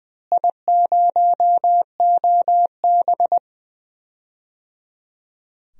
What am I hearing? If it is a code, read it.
Morse code, heard as a keyed tone, about 20 wpm, 700 Hz. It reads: I0OB
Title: DXCC-100 Call Signs - 4 Characters - Repeated 3x - 20wpm